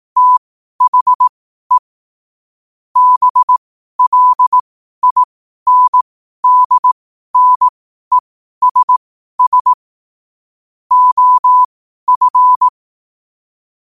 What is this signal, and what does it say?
Morse code, heard as a keyed tone, about 18 wpm, spaced as for 12 wpm, 1000 Hz. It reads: THE BLINDNESS OF
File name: 12wpm